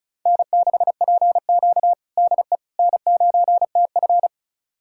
Morse code, heard as a keyed tone, about 36 wpm, 700 Hz. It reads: N6PQ DE N9TF
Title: USA Callsign DE Callsign - 4 Characters - 35wpm